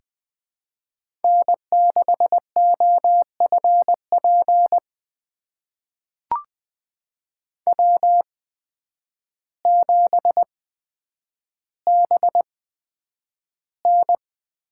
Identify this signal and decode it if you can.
Morse code, heard as a keyed tone, about 20 wpm, spaced as for 9 wpm, 700 Hz. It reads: N6OFP W7BN
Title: USA Call Signs - 8x Character Spacing - 20wpm